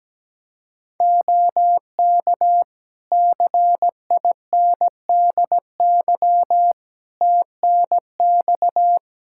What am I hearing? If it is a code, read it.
Morse code, heard as a keyed tone, about 17 wpm, 700 Hz. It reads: OK CINDY TNX
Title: K1USN SST QSO Practice 17wpm